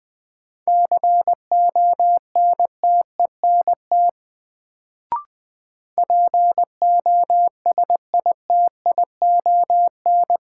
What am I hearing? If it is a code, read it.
Morse code, heard as a keyed tone, about 20 wpm, 700 Hz. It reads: CONTENT POSITION
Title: Sentences from Top 900 Words - New Word Drill - 20wpm